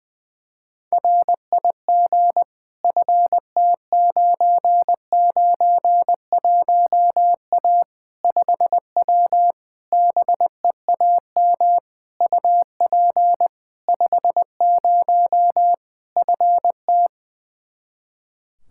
Morse code, heard as a keyed tone, about 20 wpm, 700 Hz. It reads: RIG FT991A 5W BEAM UP 50 FT
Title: Practice QSO - Exchange 2 - Info - 20wpm